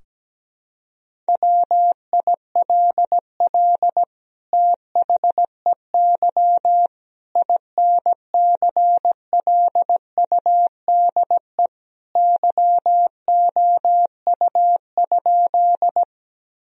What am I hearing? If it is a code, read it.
Morse code, heard as a keyed tone, about 17 wpm, 700 Hz. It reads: WILL THEY INCLUDE YOU?
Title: Sentences from Top 1000 Words - Review Part 3 - 17wpm